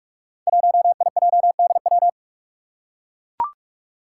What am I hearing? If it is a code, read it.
Morse code, heard as a keyed tone, about 41 wpm, 700 Hz. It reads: 1IJDW
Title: Five Challenging Letter-Number Combinations 45wpm